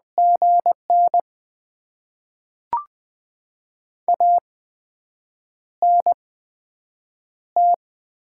Morse code, heard as a keyed tone, about 20 wpm, spaced as for 7 wpm, 700 Hz. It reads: GN ANT
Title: Top 20 QSO Elements - 8x Character Spacing - 20wpm